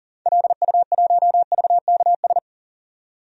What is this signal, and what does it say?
Morse code, heard as a keyed tone, about 40 wpm, 700 Hz. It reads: LU1VKS
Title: DXCC-100 Call Signs - 6 Characters - 40wpm